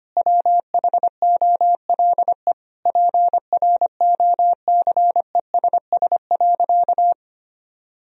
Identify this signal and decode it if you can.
Morse code, heard as a keyed tone, about 25 wpm, 700 Hz. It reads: WHOLE PROCESS.
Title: Sentences from Top 1000 Words - Review Part 3 - 25wpm